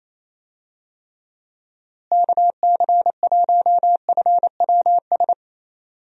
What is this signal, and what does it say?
Morse code, heard as a keyed tone, about 28 wpm, 700 Hz. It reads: KC1FWS
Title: USA Call Signs - 6 Characters - Repeated 2x - 28wpm